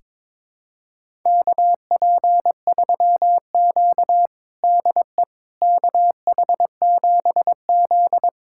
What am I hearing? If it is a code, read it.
Morse code, heard as a keyed tone, about 22 wpm, 700 Hz. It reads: KP3Q DE KH7Z
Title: USA Callsign DE Callsign - 4 Characters - 22wpm